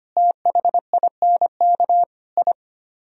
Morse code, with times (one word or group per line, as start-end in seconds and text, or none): 0.2-2.0 THINK
2.4-2.5 I